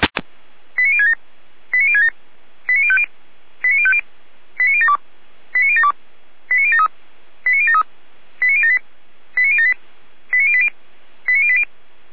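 Five-tone selective call (ZVEI), heard as quick runs of five tones, tones of about 70 ms. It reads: 89067 89067 89050 89050 89072 89072 89073 89073 89078 89078 89080 89080
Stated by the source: ZVEI five-tone selective call, address 89765, received on SDR by Feuerwehr Orlamünde.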